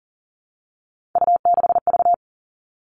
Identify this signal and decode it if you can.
Morse code, heard as a keyed tone, about 40 wpm, 700 Hz. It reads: U6V
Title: All Cut Numbers - 3-Digit 40wpm